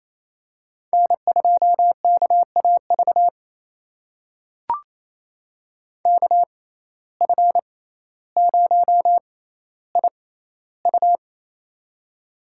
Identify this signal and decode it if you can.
Morse code, heard as a keyed tone, about 28 wpm, spaced as for 14 wpm, 700 Hz. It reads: N2KAV KF0IU
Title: USA Call Signs - 5 Characters - 6x Character Spacing - 28wpm